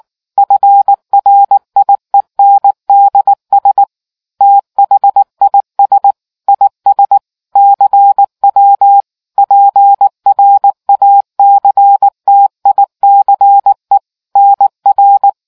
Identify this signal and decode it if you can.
Morse code, heard as a keyed tone, about 19 wpm, 800 Hz. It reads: FRIENDS THIS IS CW PRACTICE NR